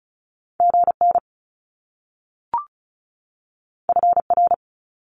Morse code, heard as a keyed tone, about 36 wpm, 700 Hz. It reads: GN FR
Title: Top Two Letter Combinations in Words 35wpm